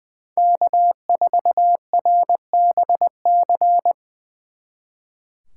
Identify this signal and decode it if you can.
Morse code, heard as a keyed tone, about 24 wpm, 700 Hz. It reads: K4RBC
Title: Speed Racing - USA Call Signs 20wpm